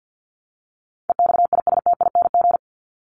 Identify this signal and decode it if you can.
Morse code, heard as a keyed tone, about 50 wpm, 700 Hz. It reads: EXISTING
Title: Top 1251-1500 Words 50wpm